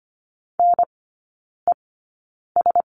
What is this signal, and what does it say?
Morse code, heard as a keyed tone, about 25 wpm, spaced as for 9 wpm, 700 Hz. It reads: NES
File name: Lesson 14e - Number 2 - Three Characters 25wpm